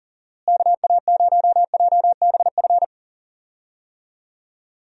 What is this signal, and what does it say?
Morse code, heard as a keyed tone, about 40 wpm, 700 Hz. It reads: KA0JBF
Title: USA Call Signs - 6 Characters - Repeated 3x - 40wpm